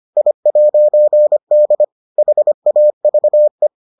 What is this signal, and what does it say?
Morse code, heard as a keyed tone, about 25 wpm, 600 Hz. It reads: I'D HAVE